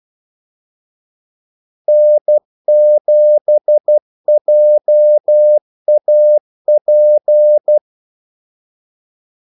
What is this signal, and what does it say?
Morse code, heard as a keyed tone, about 12 wpm, 600 Hz. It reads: N7JAP
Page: 2 ekstra ordmellemrum mellem kaldesignalerne